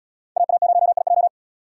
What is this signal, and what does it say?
Morse code, heard as a keyed tone, about 69 wpm, 700 Hz. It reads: AA0IJ